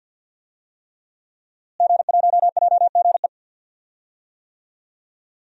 Morse code, heard as a keyed tone, about 51 wpm, 700 Hz. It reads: G1JGE
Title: DXCC-100 Call Signs - 5 Characters - Repeated 3x - 50wpm